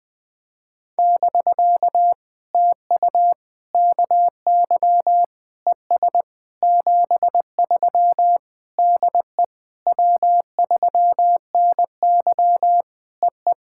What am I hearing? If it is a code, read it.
Morse code, heard as a keyed tone, about 20 wpm, 700 Hz. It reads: <BK> TU KY ES 73 DE W3NY EE
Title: POTA QSO Practice - 20wpm